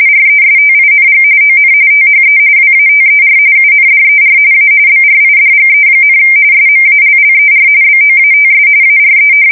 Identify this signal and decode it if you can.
RTTY, about 45.45 baud. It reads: RESTING ON THE WATER UNDER A BRIDGE, SMALLER BOATS ARE
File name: fsk170 / USS-Massachusetts